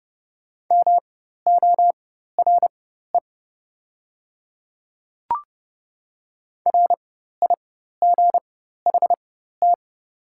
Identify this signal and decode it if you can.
Morse code, heard as a keyed tone, about 29 wpm, spaced as for 14 wpm, 700 Hz. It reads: MORE RIGHT
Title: 4x Character Spacing - Top 300 Words - 30wpm